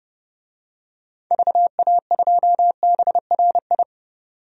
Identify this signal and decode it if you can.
Morse code, heard as a keyed tone, about 32 wpm, 700 Hz. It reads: VA2BRI
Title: DXCC-100 Call Signs - 6 Characters - 30wpm